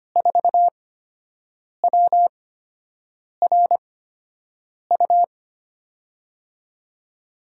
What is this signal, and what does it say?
Morse code, heard as a keyed tone, about 25 wpm, spaced as for 7 wpm, 700 Hz. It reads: 4WRU
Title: USA Call Signs - 6 Characters - 8x Character Spacing - 25wpm